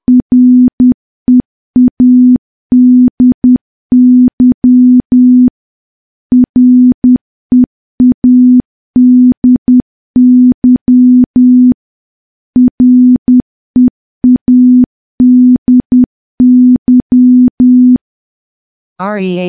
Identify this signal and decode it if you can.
Morse code, heard as a keyed tone, about 10 wpm, 250 Hz. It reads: READY READY READY
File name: morse / morse-training-10WPM